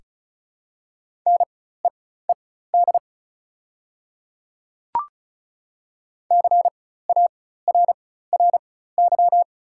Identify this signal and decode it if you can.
Morse code, heard as a keyed tone, about 36 wpm, spaced as for 16 wpm, 700 Hz. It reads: NEED CARRY
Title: 4x Character Spacing - Top 500 Words - 35wpm